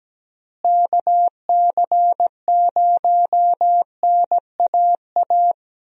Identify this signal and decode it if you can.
Morse code, heard as a keyed tone, about 17 wpm, 700 Hz. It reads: KC0NAA
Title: USA Call Signs - 6 Characters - Repeated 3x - 17wpm